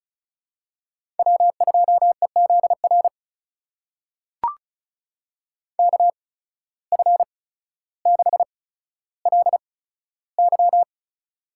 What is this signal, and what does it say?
Morse code, heard as a keyed tone, about 36 wpm, spaced as for 15 wpm, 700 Hz. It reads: W2EZR KF6LY
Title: USA Call Signs - 5 Characters - 8x Character Spacing - 35wpm